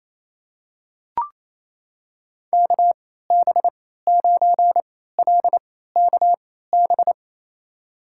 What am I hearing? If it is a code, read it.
Morse code, heard as a keyed tone, about 28 wpm, spaced as for 16 wpm, 700 Hz. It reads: KB9LKB
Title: USA Call Signs - 6 Characters - 3x Character Spacing - 28wpm